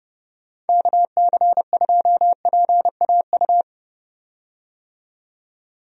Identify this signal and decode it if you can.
Morse code, heard as a keyed tone, about 31 wpm, 700 Hz. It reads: KC2PAU
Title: USA Call Signs - 6 Characters - 30wpm